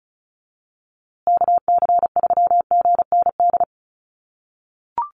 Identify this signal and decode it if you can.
Morse code, heard as a keyed tone, about 34 wpm, 700 Hz. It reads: KC3GND
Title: USA Call Signs Repeated 2x 35wpm